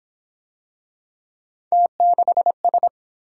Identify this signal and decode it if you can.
Morse code, heard as a keyed tone, about 26 wpm, 700 Hz. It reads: T6S
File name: Speed-Racing - Three Letter-Number Combinations - 22wpm